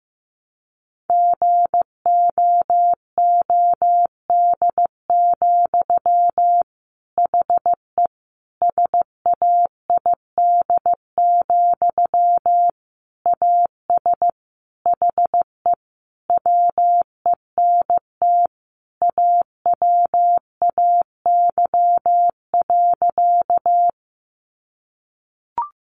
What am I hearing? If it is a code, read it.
Morse code, heard as a keyed tone, about 14 wpm, 700 Hz. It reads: GOOD, HE SAID, AS HE WENT AWAY.
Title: Sentences from Top 600 Words 15wpm